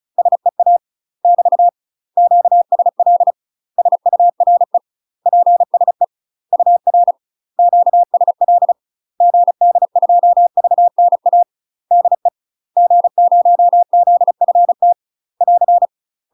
Morse code, heard as a keyed tone, about 36 wpm, 700 Hz. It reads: SEA <BT> QSL SURE PSE UR QSL GD2VNA DE G0ZFT <AR>
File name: UK MASTER QSOs TRAINING / 35WPM / UK MASTER QSOs 0023